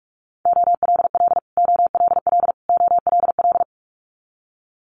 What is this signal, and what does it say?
Morse code, heard as a keyed tone, about 46 wpm, 700 Hz. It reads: OLL OLL OLL
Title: Multi-letter Phonemes Repeated 3x 45wpm